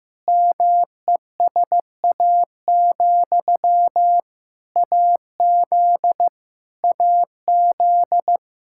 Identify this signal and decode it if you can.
Morse code, heard as a keyed tone, about 15 wpm, 700 Hz. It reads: MESA, AZ AZ
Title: Practice QSO - Exchange 1 - Beginning - 15wpm